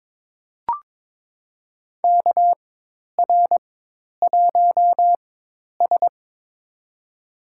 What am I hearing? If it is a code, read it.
Morse code, heard as a keyed tone, about 22 wpm, spaced as for 10 wpm, 700 Hz. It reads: KR1S
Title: USA Call Signs - 4x Character Spacing - 22wpm